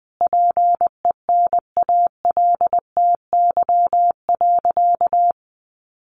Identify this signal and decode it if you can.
Morse code, heard as a keyed tone, about 20 wpm, 700 Hz. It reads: PENALTY.
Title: A Sound Of Thunder 20wpm